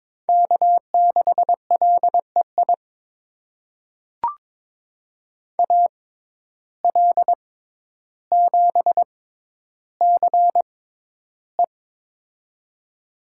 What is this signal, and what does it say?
Morse code, heard as a keyed tone, about 22 wpm, spaced as for 11 wpm, 700 Hz. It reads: K6LEI AL7CE
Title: USA Call Signs - 5 Characters - 6x Character Spacing - 22wpm